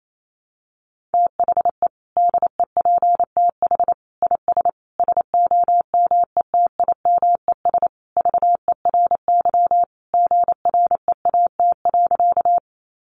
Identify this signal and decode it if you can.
Morse code, heard as a keyed tone, about 28 wpm, 700 Hz. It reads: THE DEPTH IS SOMETIMES VERY GREAT.
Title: Sentences from Top 1200 Words 28wpm